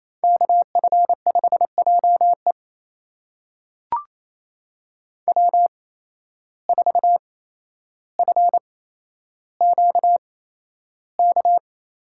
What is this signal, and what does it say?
Morse code, heard as a keyed tone, about 28 wpm, spaced as for 12 wpm, 700 Hz. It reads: KF5JE W4FQK
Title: USA Call Signs - 5 Characters - 8x Character Spacing - 28wpm